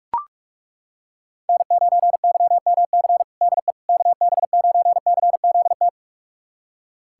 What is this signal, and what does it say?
Morse code, heard as a keyed tone, about 41 wpm, 700 Hz. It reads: N9YMC DE KB9CZT
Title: USA Callsign DE Callsign 45wpm